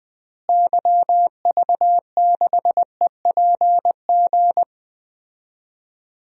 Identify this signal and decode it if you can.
Morse code, heard as a keyed tone, about 21 wpm, 700 Hz. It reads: YV6EPG
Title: DXCC-100 Call Signs - 6 Characters - Repeated 2x - 20wpm